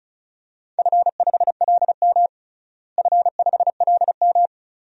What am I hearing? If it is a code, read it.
Morse code, heard as a keyed tone, about 34 wpm, 700 Hz. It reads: F5LM F5LM
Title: DXCC-100 Call Signs - 4 Characters - Repeated 3x - 35wpm